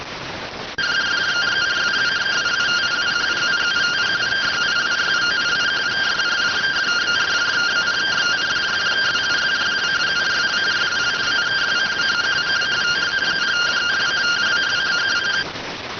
Two-tone teleprinter signal, about 170 Hz shift.